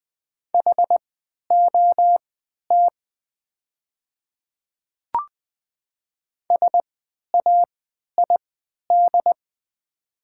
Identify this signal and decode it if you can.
Morse code, heard as a keyed tone, about 20 wpm, spaced as for 11 wpm, 690 Hz. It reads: HOT SAID
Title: 3x Character Spacing - Top 100 Words - 20wpm